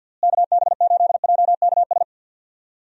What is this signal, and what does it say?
Morse code, heard as a keyed tone, about 48 wpm, 700 Hz. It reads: KB8JKS
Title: USA Call Signs - 6 Characters - Repeated 2x - 50wpm